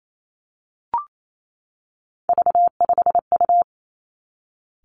Morse code, heard as a keyed tone, about 28 wpm, 700 Hz. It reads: V5U
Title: Three Challenging Letter-Number Combinations 28wpm